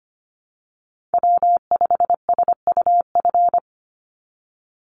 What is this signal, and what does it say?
Morse code, heard as a keyed tone, about 25 wpm, 700 Hz. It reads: W5SUF
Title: USA Call Signs 25wpm